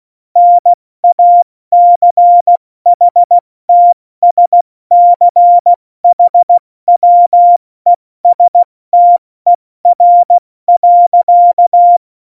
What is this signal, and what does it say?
Morse code, heard as a keyed tone, about 16 wpm, 700 Hz. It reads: NACHTSCHWESTER.
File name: Morsekurs / Witze / Witz 12. Dezember 2017, 70 F80